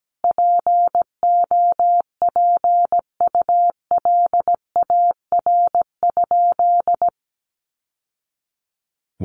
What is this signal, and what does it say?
Morse code, heard as a keyed tone, about 17 wpm, 700 Hz. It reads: POPULAR?
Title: Sentences from Top 1200 Words 17wpm